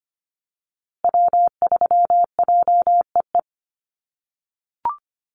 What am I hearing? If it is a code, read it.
Morse code, heard as a keyed tone, about 25 wpm, 700 Hz. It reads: W3JEE
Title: USA Call Signs Repeated 2x 25wpm